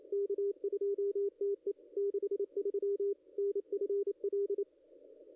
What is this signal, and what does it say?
Morse code, heard as a keyed tone, about 27 wpm, 440 Hz, with radio noise. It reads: K2TE 63 NFL
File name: contests / ss-cw-k5tr-op-n5rz -0100